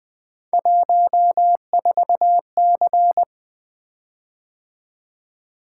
Morse code, heard as a keyed tone, about 20 wpm, 700 Hz. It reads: 14C
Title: Three Letter-Number Combinations - 20wpm